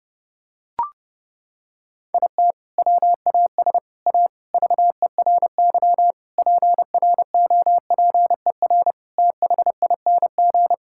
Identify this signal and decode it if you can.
Morse code, heard as a keyed tone, about 30 wpm, 700 Hz. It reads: IT WAS A VERY PROPER THING
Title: Sentences from Top 1000 Words - Review Part 5 - 30wpm